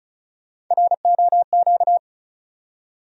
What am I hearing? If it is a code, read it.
Morse code, heard as a keyed tone, about 36 wpm, 700 Hz. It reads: ROQ
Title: USA Call Sign Postfixes - 3 Letters 35wpm